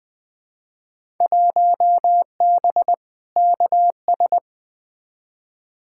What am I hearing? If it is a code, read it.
Morse code, heard as a keyed tone, about 20 wpm, 700 Hz. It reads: 1B KS
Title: Field Day Exchanges Encore - 20wpm